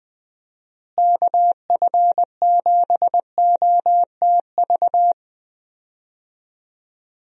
Morse code, heard as a keyed tone, about 21 wpm, 700 Hz. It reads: KF7OTV
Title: USA Call Signs - 6 Characters - Repeated 2x - 20wpm